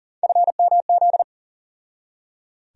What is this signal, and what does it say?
Morse code, heard as a keyed tone, about 37 wpm, 700 Hz. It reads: FMZ
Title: USA Call Sign Postfixes - 3 Letters 40wpm